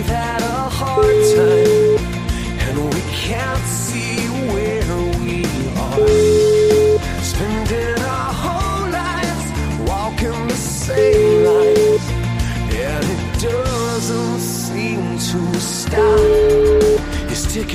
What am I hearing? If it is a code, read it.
Ringback tone (Europe, 425 Hz), with noise.